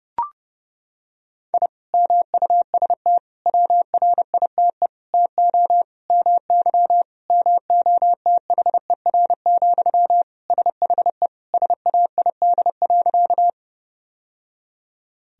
Morse code, heard as a keyed tone, about 30 wpm, 700 Hz. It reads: I MUST WRITE TO MY MOTHER, SHE SAID.
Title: Sentences from Top 1000 Words - Review Part 1 - 30wpm